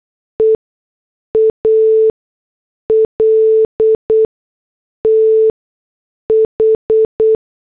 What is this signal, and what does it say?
Morse code, heard as a keyed tone, about 8 wpm, spaced as for 6 wpm, 440 Hz. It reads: EALTH